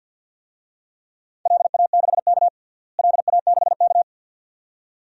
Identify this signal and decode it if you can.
Morse code, heard as a keyed tone, about 51 wpm, 700 Hz. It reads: LA6K LA6K
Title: DXCC-100 Call Signs - 4 Characters - Repeated 2x - 50wpm